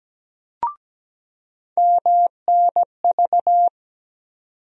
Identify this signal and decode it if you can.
Morse code, heard as a keyed tone, about 17 wpm, 700 Hz. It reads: MNV
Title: USA Call Sign Postfixes - 3 Letters 17wpm